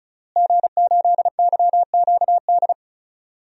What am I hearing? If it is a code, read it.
Morse code, heard as a keyed tone, about 36 wpm, 700 Hz. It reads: G8YQD
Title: DXCC-100 Call Signs - 5 Characters - Repeated 2x - 35wpm